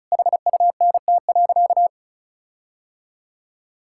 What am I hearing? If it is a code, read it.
Morse code, heard as a keyed tone, about 34 wpm, 700 Hz. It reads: HUNT.
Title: Sentences from Top 1600 Words 35wpm